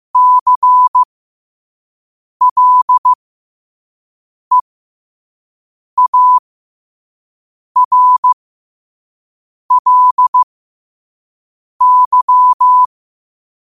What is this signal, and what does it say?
Morse code, heard as a keyed tone, about 15 wpm, spaced as for 5 wpm, 1000 Hz. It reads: CLEARLY
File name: qotdfiles / 5wpm